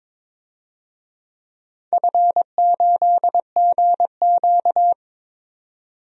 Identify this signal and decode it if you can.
Morse code, heard as a keyed tone, about 22 wpm, 700 Hz. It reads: F8GQ
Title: DXCC-100 Call Signs - 4 Characters - 22wpm